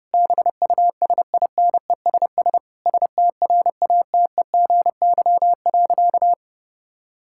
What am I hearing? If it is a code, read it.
Morse code, heard as a keyed tone, about 30 wpm, 700 Hz. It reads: BUSINESS STRATEGY.
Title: Sentences from Top 1000 Words - Review Part 5 - 30wpm